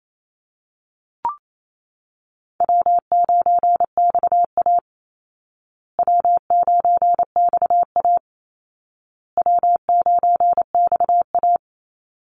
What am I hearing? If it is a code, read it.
Morse code, heard as a keyed tone, about 28 wpm, 700 Hz. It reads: W9XA W9XA W9XA
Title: Easy Call Signs Repeated 3x - 28wpm